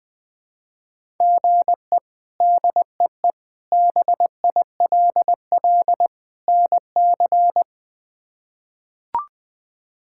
Morse code, heard as a keyed tone, about 20 wpm, 700 Hz. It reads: GE DEE BILL NC
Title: K1USN SST QSO Practice - 20wpm